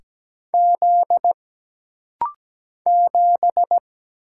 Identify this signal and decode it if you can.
Morse code, heard as a keyed tone, about 17 wpm, 700 Hz. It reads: Z 7
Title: Single Letter-Number - Rapid-Fire 17wpm